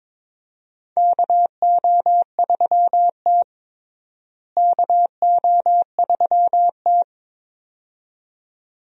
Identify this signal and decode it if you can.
Morse code, heard as a keyed tone, about 22 wpm, 700 Hz. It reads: KO3T KO3T
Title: USA Call Signs - 4 Characters - Repeated 2x - 22wpm